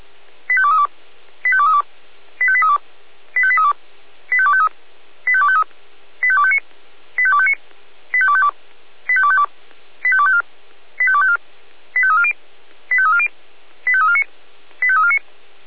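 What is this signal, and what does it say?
Five-tone selective call (ZVEI), heard as quick runs of five tones, tones of about 70 ms. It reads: 85232 85232 85832 85832 85253 85253 85259 85259 85252 85252 85254 85254 85340 85340 85349 85349
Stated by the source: ZVEI five-tone selective call 85320 received by SDR.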